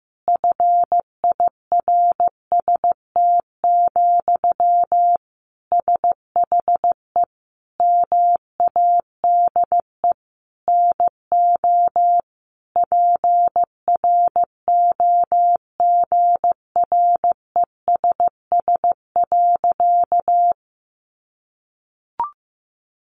Morse code, heard as a keyed tone, about 15 wpm, 700 Hz. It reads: FIRST, SHE MADE NO PROGRESS.
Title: Sentences from Top 1400 Words 15wpm